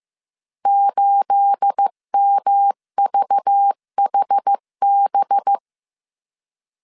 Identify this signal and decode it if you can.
Morse code, heard as a keyed tone, about 15 wpm, 800 Hz. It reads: 8MVHB
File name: CW / DET / 1 9 1500C S B 14wpm 3302